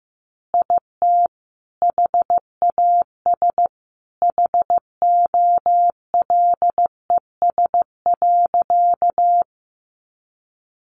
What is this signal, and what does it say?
Morse code, heard as a keyed tone, about 15 wpm, 700 Hz. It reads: IT HAS HOLES.
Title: Additional Easy Phrases 15wpm